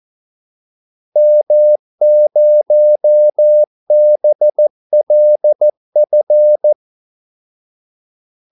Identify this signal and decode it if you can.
Morse code, heard as a keyed tone, about 14 wpm, 600 Hz. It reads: M0BLF
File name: Callsigns-14wpm